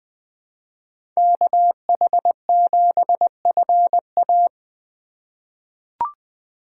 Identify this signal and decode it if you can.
Morse code, heard as a keyed tone, about 20 wpm, 700 Hz. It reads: KH7FA
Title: USA Call Signs - 5 Characters - 4x Character Spacing - 20wpm